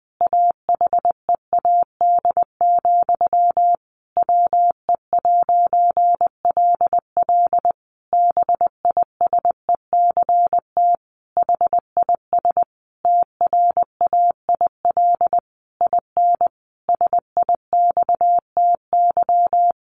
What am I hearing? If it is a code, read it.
Morse code, heard as a keyed tone, about 20 wpm, 700 Hz. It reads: AHEAD, WE'LL BISECT HIS TRAIL IN SIXTY
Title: A Sound Of Thunder 20wpm